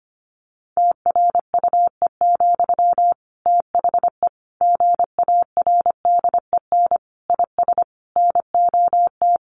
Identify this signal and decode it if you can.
Morse code, heard as a keyed tone, about 25 wpm, 700 Hz. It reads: TRUE, THE GARDEN IS NOT
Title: Sentences from Top 700 Words 25wpm